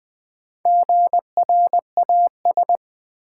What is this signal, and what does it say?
Morse code, heard as a keyed tone, about 20 wpm, 700 Hz. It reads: GRAS
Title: Build A Word - 1 Letter At A Time - 5-Letter Words - 20wpm